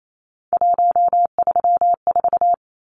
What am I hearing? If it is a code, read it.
Morse code, heard as a keyed tone, about 28 wpm, 700 Hz. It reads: 134
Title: Numbers - 3-Digit 28wpm